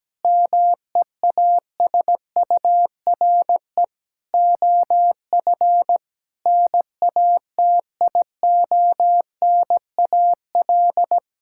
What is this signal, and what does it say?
Morse code, heard as a keyed tone, about 17 wpm, 700 Hz. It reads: MEASURE OF NATIONAL
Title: Sentences from Top 1600 Words 17wpm